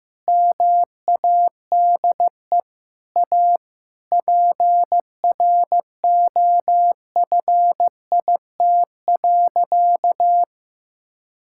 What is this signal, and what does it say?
Morse code, heard as a keyed tone, about 15 wpm, 700 Hz. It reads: MADE A PROFIT.
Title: Sentences from Top 1000 Words - Review Part 5 - 15wpm